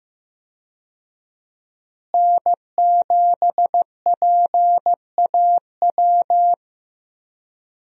Morse code, heard as a keyed tone, about 15 wpm, 700 Hz. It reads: N7PAW
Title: USA Call Signs - 5 Characters - Repeated 3x - 15wpm